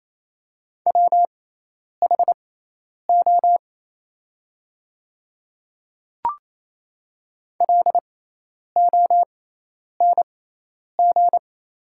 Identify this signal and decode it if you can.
Morse code, heard as a keyed tone, about 28 wpm, spaced as for 10 wpm, 700 Hz. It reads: WHO LONG
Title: 6x Character Spacing - Top 200 Words - 28wpm